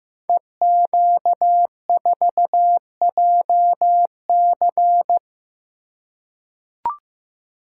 Morse code, heard as a keyed tone, about 15 wpm, 700 Hz. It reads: EQ4JC
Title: Code Groups - 5 Letter:Numbers - 15wpm